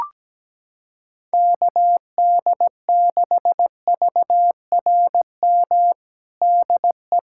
Morse code, heard as a keyed tone, about 17 wpm, 700 Hz. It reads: KD6VRM DE
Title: USA Callsign DE Callsign 17wpm